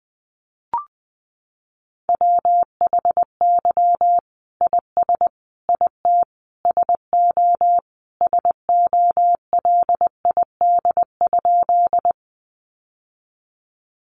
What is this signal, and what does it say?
Morse code, heard as a keyed tone, about 20 wpm, 700 Hz. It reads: WHY IS IT SO SOLID?
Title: Sentences from Top 1500 Words 20wpm